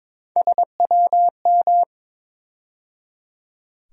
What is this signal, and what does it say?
Morse code, heard as a keyed tone, about 22 wpm, 700 Hz. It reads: SWM
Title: USA Call Sign Postfixes - 3 Letters 22wpm